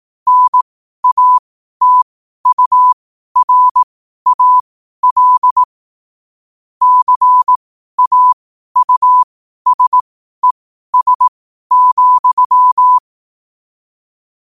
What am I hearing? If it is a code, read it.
Morse code, heard as a keyed tone, about 18 wpm, spaced as for 12 wpm, 1000 Hz. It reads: NATURAL CAUSES,
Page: Quotes for Sun, 27 Apr 2025 in Morse Code at 12 words per minute.